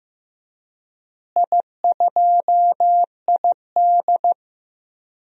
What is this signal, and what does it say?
Morse code, heard as a keyed tone, about 15 wpm, 700 Hz. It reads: I2ID
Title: DXCC-100 Call Signs - 4 Characters - Repeated 3x - 15wpm